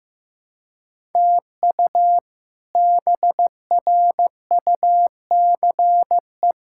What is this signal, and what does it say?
Morse code, heard as a keyed tone, about 15 wpm, 700 Hz. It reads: TU BRUCE
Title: K1USN SST QSO Practice - 15wpm